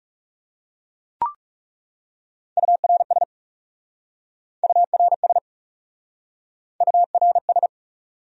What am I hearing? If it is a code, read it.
Morse code, heard as a keyed tone, about 42 wpm, 700 Hz. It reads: URS URS URS